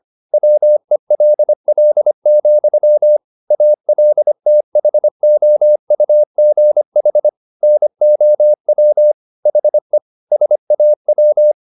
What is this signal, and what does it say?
Morse code, heard as a keyed tone, about 25 wpm, 600 Hz. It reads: WELL, ALTHOUGH NOW HE SAW